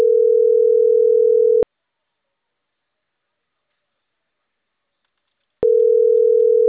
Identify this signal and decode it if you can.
Ringback tone (North America), 440 + 480 Hz.